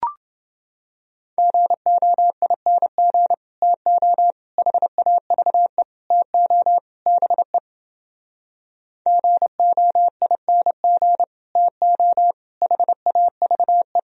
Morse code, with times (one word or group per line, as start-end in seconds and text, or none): 1.4-3.3 GOING
3.6-4.3 TO
4.6-5.8 HAVE
6.1-6.8 TO
7.1-7.6 BE
9.1-11.2 GOING
11.6-12.3 TO
12.6-14.0 HAVE